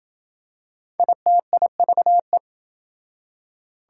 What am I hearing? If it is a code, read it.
Morse code, heard as a keyed tone, about 27 wpm, 700 Hz. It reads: ITIVE